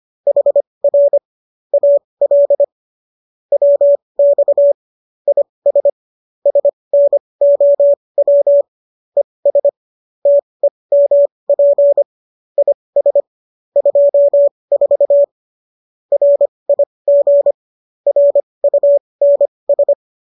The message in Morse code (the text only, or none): HR AL WX IS SNOW ES TEMP IS 24 RIG RUNS